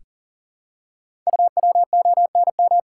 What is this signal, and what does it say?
Morse code, heard as a keyed tone, about 41 wpm, 690 Hz. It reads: UWONM
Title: Code Groups - 5 Letters - 40wpm